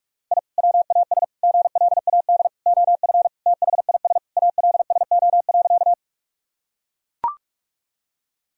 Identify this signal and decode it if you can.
Morse code, heard as a keyed tone, about 42 wpm, 700 Hz. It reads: I WAS GLAD OF THIS ALSO.
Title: Sentences from Top 1000 Words - Review Part 1 - 45wpm